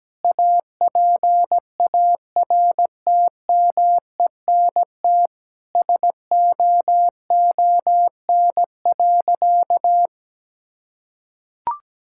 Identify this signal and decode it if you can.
Morse code, heard as a keyed tone, about 17 wpm, 700 Hz. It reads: APARTMENT SOON.
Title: Sentences from Top 1000 Words - Review Part 6 - 17wpm